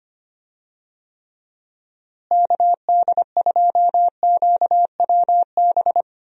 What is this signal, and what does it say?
Morse code, heard as a keyed tone, about 25 wpm, 700 Hz. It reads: KD2QWB
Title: USA Call Signs - 6 Characters - Repeated 3x - 25wpm